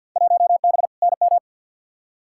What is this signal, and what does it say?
Morse code, heard as a keyed tone, about 53 wpm, 700 Hz. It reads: 1B NM
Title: Field Day Exchanges Encore - 50wpm